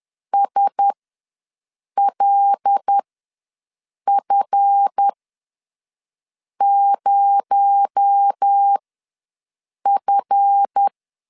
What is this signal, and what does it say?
Morse code, heard as a keyed tone, about 10 wpm, spaced as for 5 wpm, 800 Hz. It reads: SLF0F
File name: LEZIONI / 7 GF9Z / 7 GF ws 10wpm 3918